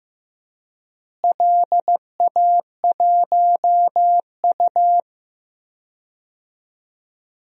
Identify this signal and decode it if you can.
Morse code, heard as a keyed tone, about 15 wpm, 700 Hz. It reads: LA1U
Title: DXCC-100 Call Signs - 4 Characters - Repeated 3x - 15wpm